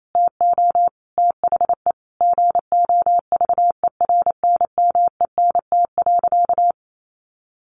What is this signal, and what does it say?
Morse code, heard as a keyed tone, about 28 wpm, 700 Hz. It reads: TO THE GOVERNMENT.
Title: A Sound Of Thunder 28wpm